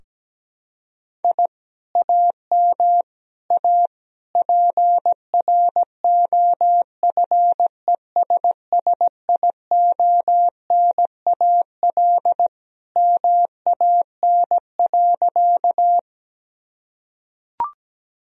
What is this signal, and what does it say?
Morse code, heard as a keyed tone, about 17 wpm, 700 Hz. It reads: I AM A PROFESSIONAL MAN.
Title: Sentences from Top 1000 Words - Review Part 4 - 17wpm